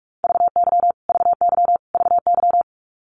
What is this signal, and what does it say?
Morse code, heard as a keyed tone, about 44 wpm, 700 Hz. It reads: VY VY VY
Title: Abbreviations Repeated 3x - 45wpm